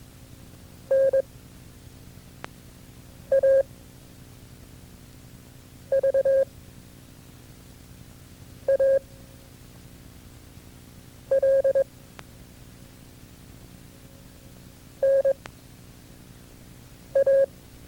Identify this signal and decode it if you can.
Morse code, heard as a keyed tone, about 22 wpm, 540 Hz. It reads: N A V A L N A
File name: FreeFiles / CW-practice / KB6MT-tape2-side1